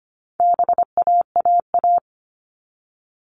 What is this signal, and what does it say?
Morse code, heard as a keyed tone, about 25 wpm, 700 Hz. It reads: BAAA
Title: All Cut Numbers - 4-Digit 25wpm